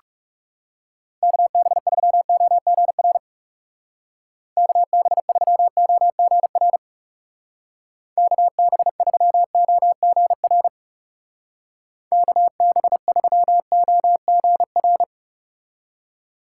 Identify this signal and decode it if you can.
Morse code, heard as a keyed tone, about 37 wpm, 700 Hz. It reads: KB3OGR KB3OGR KB3OGR KB3OGR
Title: Speed Racing - USA 2x3 Call Signs 30wpm